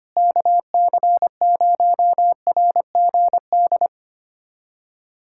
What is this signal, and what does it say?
Morse code, heard as a keyed tone, about 25 wpm, 700 Hz. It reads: KC0RGD
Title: USA Callsign DE Callsign - 6 Characters - 25wpm